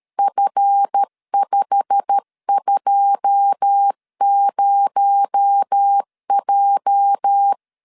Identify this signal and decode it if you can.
Morse code, heard as a keyed tone, about 13 wpm, 800 Hz. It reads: F520J
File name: LEZIONI / 9 2 12wpm 2301